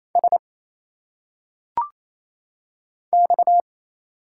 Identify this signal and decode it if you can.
Morse code, heard as a keyed tone, about 28 wpm, 700 Hz. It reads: S X
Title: Single Letters - Phonetic Pronunciation 28wpm